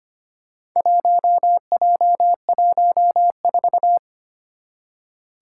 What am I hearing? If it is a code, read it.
Morse code, heard as a keyed tone, about 25 wpm, 700 Hz. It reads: 1J14
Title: Four Challenging Letter-Number Combinations 25wpm